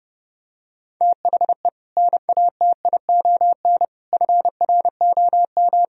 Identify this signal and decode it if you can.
Morse code, heard as a keyed tone, about 30 wpm, 700 Hz. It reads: THE NATION FROM